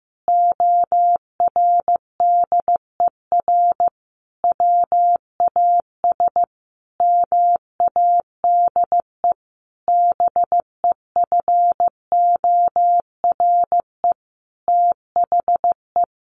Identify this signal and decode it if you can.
Morse code, heard as a keyed tone, about 15 wpm, 700 Hz. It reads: ORDER WAS MADE BEFORE THE
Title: Sentences from Top 1500 Words 15wpm